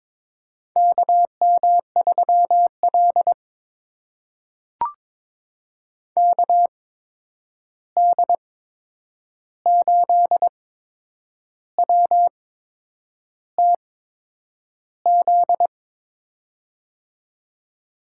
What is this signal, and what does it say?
Morse code, heard as a keyed tone, about 21 wpm, spaced as for 8 wpm, 700 Hz. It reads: KM3L KD8WTZ
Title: USA Call Signs - 8x Character Spacing - 22wpm